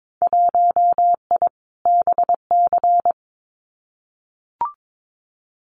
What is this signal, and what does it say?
Morse code, heard as a keyed tone, about 22 wpm, 700 Hz. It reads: 1I BC
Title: Winter Field Day Exchanges Encore 22wpm